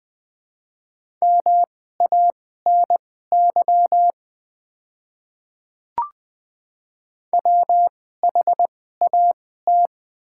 Morse code, heard as a keyed tone, about 20 wpm, spaced as for 15 wpm, 700 Hz. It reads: MANY WHAT
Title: 2x Character Spacing - Top 100 Words - 20wpm